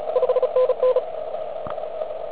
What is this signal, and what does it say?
Morse code, heard as a keyed tone, about 32 wpm, 510 Hz, with radio noise. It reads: 5NN